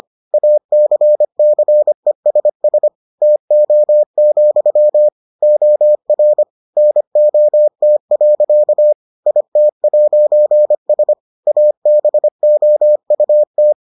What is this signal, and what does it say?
Morse code, heard as a keyed tone, about 25 wpm, 600 Hz. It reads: ACCESS TO, OR NOT. IT'S ABOUT